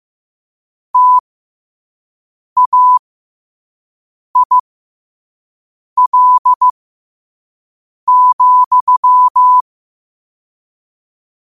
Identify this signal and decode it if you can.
Morse code, heard as a keyed tone, about 15 wpm, spaced as for 5 wpm, 1000 Hz. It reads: TAIL,
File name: 5wpm